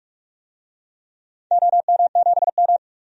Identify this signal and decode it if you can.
Morse code, heard as a keyed tone, about 44 wpm, 700 Hz. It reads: OM7M
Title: DXCC-100 Call Signs - 4 Characters - 45wpm